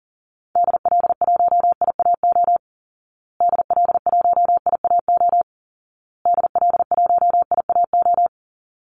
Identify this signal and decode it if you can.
Morse code, heard as a keyed tone, about 40 wpm, 700 Hz. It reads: DL1IAO DL1IAO DL1IAO
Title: Call Signs Repeated 2x - 40wpm